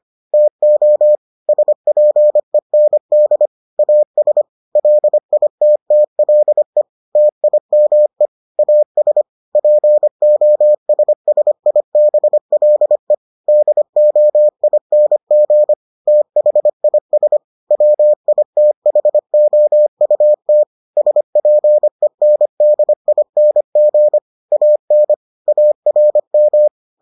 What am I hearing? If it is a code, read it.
Morse code, heard as a keyed tone, about 25 wpm, 600 Hz. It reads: TO SPEND AS LITTLE TIME AS POSSIBLE DOING THIS WITHOUT SPENDING AN ARM